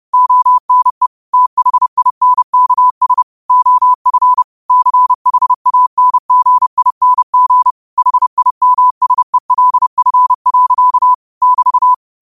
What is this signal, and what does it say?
Morse code, heard as a keyed tone, about 30 wpm, 1000 Hz. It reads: ONE THINKS OF CHANGING HIMSELF. <BT>
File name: 2026-02-23-30wpm